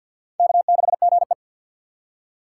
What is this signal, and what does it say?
Morse code, heard as a keyed tone, about 53 wpm, 700 Hz. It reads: K6GE
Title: USA Call Signs - 6x Character Spacing - 50wpm